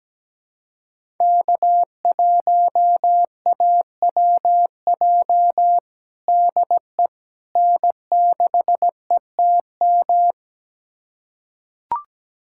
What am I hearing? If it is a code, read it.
Morse code, heard as a keyed tone, about 17 wpm, 700 Hz. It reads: K1AWJ DE N6ETM
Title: USA Callsign DE Callsign 17wpm